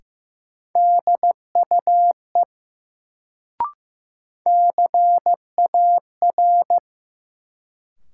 Morse code, heard as a keyed tone, about 15 wpm, 700 Hz. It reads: DUE CAR
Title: Sets of 5 Words Encore - New Word Drill - 15wpm